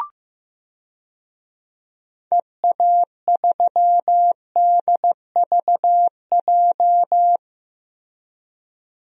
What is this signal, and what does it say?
Morse code, heard as a keyed tone, about 15 wpm, 700 Hz. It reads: EA3DVJ
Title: DXCC-100 Call Signs - 6 Characters - 15wpm